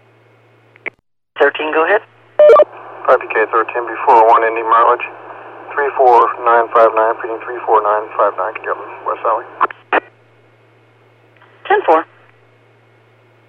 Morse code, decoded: none

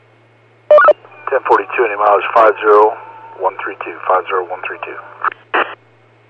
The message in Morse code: none

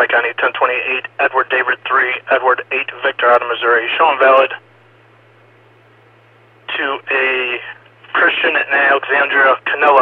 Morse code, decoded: none

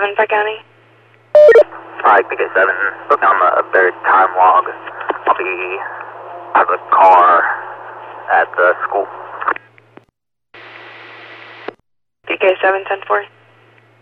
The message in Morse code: none

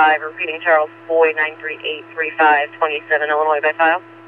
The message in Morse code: none

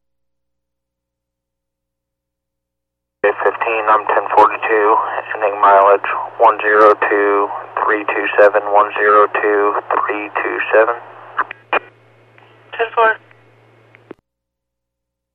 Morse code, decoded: none

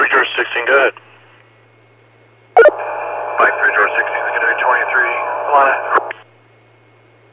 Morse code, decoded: none

